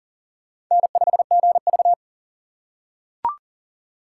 Morse code, heard as a keyed tone, about 40 wpm, 700 Hz. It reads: N5GV
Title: USA Call Signs - 6x Character Spacing - 40wpm